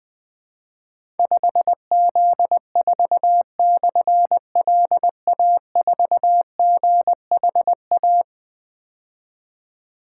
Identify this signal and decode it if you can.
Morse code, heard as a keyed tone, about 20 wpm, 700 Hz. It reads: 5Z4/LA4GHA
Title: Difficult Call Signs 20wpm